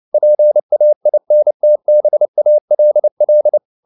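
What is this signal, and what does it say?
Morse code, heard as a keyed tone, about 29 wpm, 600 Hz. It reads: PAINTBALL
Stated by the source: Morse code at 29 wpm, sending Ken Follett's novel Never in English, from CW med Gnister.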